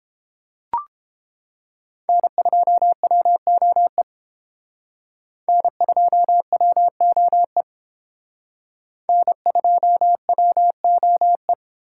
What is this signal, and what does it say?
Morse code, heard as a keyed tone, about 31 wpm, 700 Hz. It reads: N2WOE N2WOE N2WOE